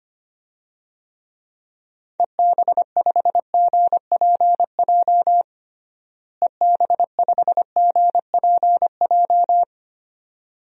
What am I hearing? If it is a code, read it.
Morse code, heard as a keyed tone, about 25 wpm, 700 Hz. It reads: EB5GPJ EB5GPJ
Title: DXCC-100 Call Signs - 6 Characters - Repeated 2x - 25wpm